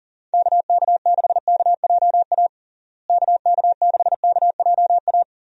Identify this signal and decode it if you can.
Morse code, heard as a keyed tone, about 37 wpm, 700 Hz. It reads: KK6KJA KK6KJA
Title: USA Call Signs - 6 Characters - Repeated 3x - 40wpm